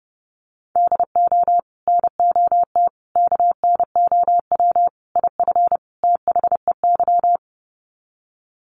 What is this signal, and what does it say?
Morse code, heard as a keyed tone, about 30 wpm, 700 Hz. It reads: DO NOT KNOW IF THEY
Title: Sets of 5 Words Encore 30wpm